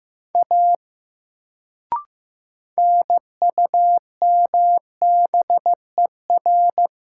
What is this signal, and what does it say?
Morse code, heard as a keyed tone, about 15 wpm, 700 Hz. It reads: A NUMBER
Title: Sentences from Top 100 Words - New Word Drill - 15wpm